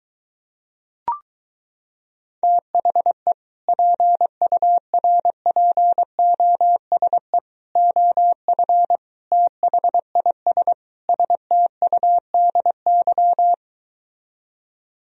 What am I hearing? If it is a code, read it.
Morse code, heard as a keyed tone, about 23 wpm, 700 Hz. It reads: THE PURPOSE OF THIS STUDY